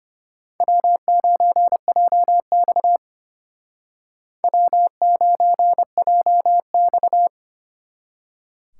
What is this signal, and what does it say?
Morse code, heard as a keyed tone, about 33 wpm, 700 Hz. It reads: W9JX W9JX